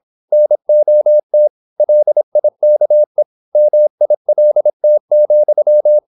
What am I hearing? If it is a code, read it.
Morse code, heard as a keyed tone, about 26 wpm, 600 Hz. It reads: NOT LIKE MILT,
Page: Never 26wpm | CW med Gnister